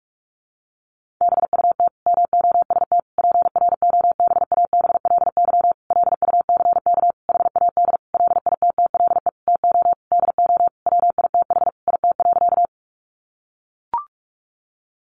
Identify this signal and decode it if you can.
Morse code, heard as a keyed tone, about 44 wpm, 700 Hz. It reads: BUT MOST PROBABLY LUCK HAD LITTLE TO DO WITH IT.
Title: Sentences from Top 1500 Words 45wpm